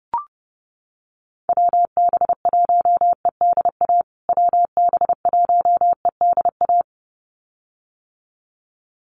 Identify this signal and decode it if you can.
Morse code, heard as a keyed tone, about 30 wpm, 700 Hz. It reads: WB1EDA WB1EDA
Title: USA Call Signs Repeated 2x 30wpm